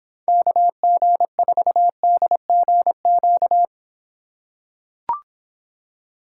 Morse code, heard as a keyed tone, about 25 wpm, 700 Hz. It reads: KG4DGQ